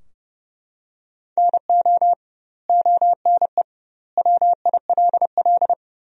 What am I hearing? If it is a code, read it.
Morse code, heard as a keyed tone, about 32 wpm, 700 Hz. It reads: NO ONE WILL
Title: 2x Word-Spacing - Sentences from Top 100 Words - 30wpm